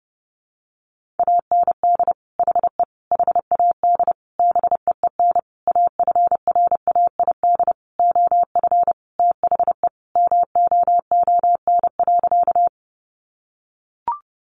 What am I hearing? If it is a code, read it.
Morse code, heard as a keyed tone, about 29 wpm, 700 Hz. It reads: AND HE HAD BEEN AFRAID OF THE MOON.
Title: Sentences from Top 1500 Words 30wpm